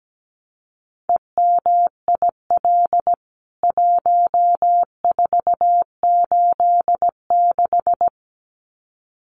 Morse code, heard as a keyed tone, about 17 wpm, 700 Hz. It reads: EMIL 1486
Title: CWT Exchange 17wpm